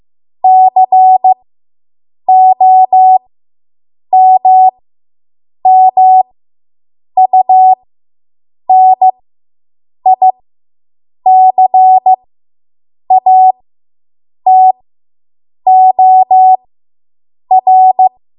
Morse code, heard as a keyed tone, about 15 wpm, spaced as for 7 wpm, 750 Hz. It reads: COMMUNICATOR